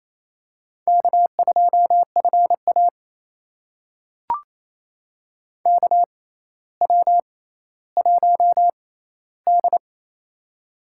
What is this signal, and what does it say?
Morse code, heard as a keyed tone, about 28 wpm, spaced as for 14 wpm, 700 Hz. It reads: K2FA KW1D
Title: USA Call Signs - 6x Character Spacing - 28wpm